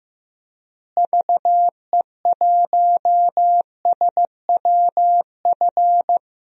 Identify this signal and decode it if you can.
Morse code, heard as a keyed tone, about 15 wpm, 700 Hz. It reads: VE1SWF
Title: DXCC-100 Call Signs - 6 Characters - Repeated 2x - 15wpm